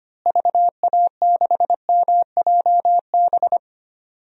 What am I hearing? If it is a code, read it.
Morse code, heard as a keyed tone, about 25 wpm, 700 Hz. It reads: VA6MJB
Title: DXCC-100 Call Signs - 6 Characters - Repeated 2x - 25wpm